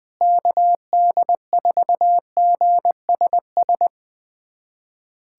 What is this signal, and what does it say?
Morse code, heard as a keyed tone, about 20 wpm, 700 Hz. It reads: KD4GSS
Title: USA Callsign DE Callsign - 6 Characters - 20wpm